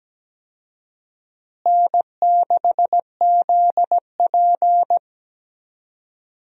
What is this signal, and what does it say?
Morse code, heard as a keyed tone, about 17 wpm, 700 Hz. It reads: N6ZP
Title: USA Call Signs - 4 Characters - Repeated 3x - 17wpm